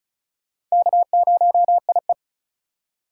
Morse code, heard as a keyed tone, about 36 wpm, 700 Hz. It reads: K0IE
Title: USA Call Signs - 4 Characters - 3x Character Spacing - 35wpm